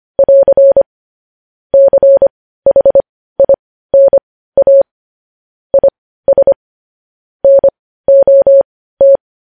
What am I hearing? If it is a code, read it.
Morse code, heard as a keyed tone, about 25 wpm, spaced as for 15 wpm, 550 Hz. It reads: <AR> CHINA IS NOT
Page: news-headlines-in-morse-code-at-15-wpm